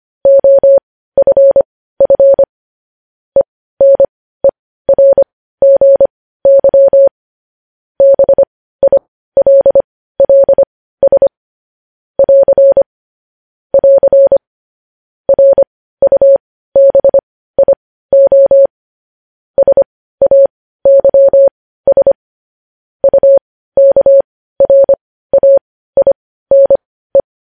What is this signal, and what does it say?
Morse code, heard as a keyed tone, about 25 wpm, spaced as for 15 wpm, 550 Hz. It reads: OFF ENERGY BILLS <AR> <AR> RUBIO SAYS UKRAINE
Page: News headlines in Morse code at 15 wpm